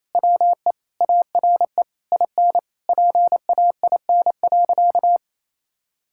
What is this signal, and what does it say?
Morse code, heard as a keyed tone, about 27 wpm, 700 Hz. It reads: WE ARE IN PAIN.
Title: Sentences from Top 1600 Words 28wpm